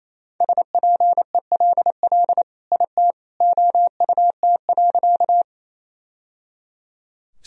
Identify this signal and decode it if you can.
Morse code, heard as a keyed tone, about 28 wpm, 700 Hz. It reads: SPELL IT OUT.
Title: Sentences from Top 1000 Words - Review Part 1 - 28wpm